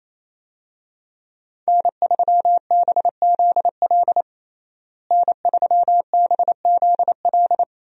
Morse code, heard as a keyed tone, about 28 wpm, 700 Hz. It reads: N3BZL N3BZL
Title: USA Call Signs - 5 Characters - Repeated 3x - 28wpm